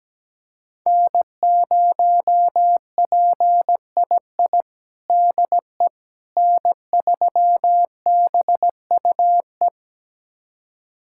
Morse code, heard as a keyed tone, about 17 wpm, 700 Hz. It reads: N0PII DE N3BUE
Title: USA Callsign DE Callsign - 5 Characters - 17wpm